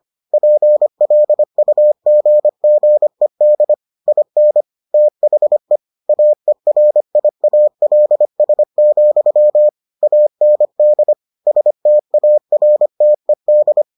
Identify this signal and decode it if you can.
Morse code, heard as a keyed tone, about 25 wpm, 600 Hz. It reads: PLUGGED IN THE AERIALS, AND STARTED